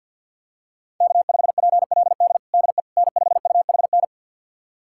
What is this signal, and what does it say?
Morse code, heard as a keyed tone, about 52 wpm, 700 Hz. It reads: K5PLD DE N5UHN
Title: USA Callsign DE Callsign - 5 Characters - 50wpm